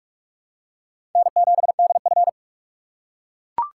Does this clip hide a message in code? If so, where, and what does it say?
Yes: Morse code from 1.2 to 2.3 s, heard as a keyed tone, about 40 wpm, 700 Hz. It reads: N7DF